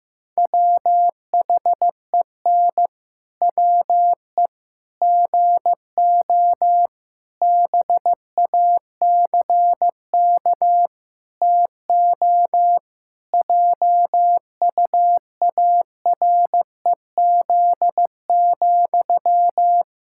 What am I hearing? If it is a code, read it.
Morse code, heard as a keyed tone, about 15 wpm, 700 Hz. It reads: WHEN WE GO BACK TO JUAREZ,
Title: Pangram Challenge v2 - 15wpm